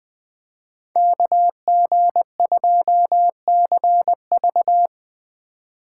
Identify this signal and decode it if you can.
Morse code, heard as a keyed tone, about 20 wpm, 700 Hz. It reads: KG2CV
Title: 2023 POTA Hunter Call Signs - 20wpm